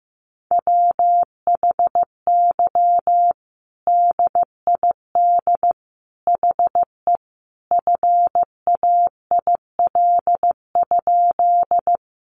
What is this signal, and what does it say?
Morse code, heard as a keyed tone, about 15 wpm, 700 Hz. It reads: WHY DID HE FAIL?
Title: Sentences from Top 1100 Words 15wpm